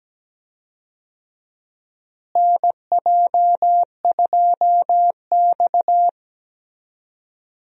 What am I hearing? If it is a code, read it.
Morse code, heard as a keyed tone, about 17 wpm, 700 Hz. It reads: NJ2X
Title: USA Call Signs - 4 Characters - Repeated 3x - 17wpm